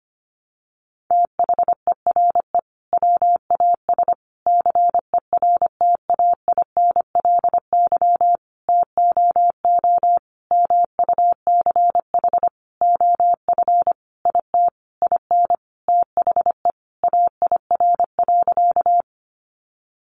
Morse code, heard as a keyed tone, about 25 wpm, 700 Hz. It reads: THERE WAS CERTAINLY TOO MUCH OF IT IN THE AIR.
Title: Sentences from Top 900 Words 25wpm